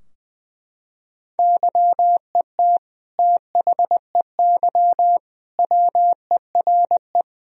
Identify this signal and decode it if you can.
Morse code, heard as a keyed tone, about 21 wpm, 700 Hz. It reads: YET THEY WERE
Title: Sentences from Top 1000 Words - Review Part 3 - 20wpm